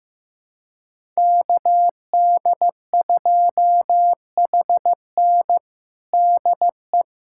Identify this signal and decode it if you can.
Morse code, heard as a keyed tone, about 16 wpm, 700 Hz. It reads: KD2HN DE
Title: USA Callsign DE Callsign - 5 Characters - 15wpm